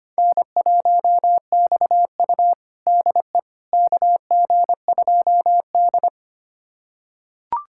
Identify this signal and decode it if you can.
Morse code, heard as a keyed tone, about 25 wpm, 700 Hz. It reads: N1XU DE KG2D
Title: USA Callsign DE Callsign - 4 Characters - 25wpm